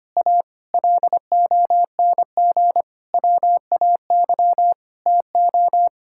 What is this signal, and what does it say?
Morse code, heard as a keyed tone, about 25 wpm, 700 Hz. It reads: A LONG WAY TO
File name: Sets of 9 Words - 25wpm